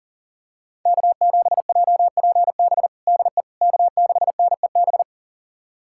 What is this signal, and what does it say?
Morse code, heard as a keyed tone, about 40 wpm, 700 Hz. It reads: K7JPB DE K6NEB
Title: USA Callsign DE Callsign 40wpm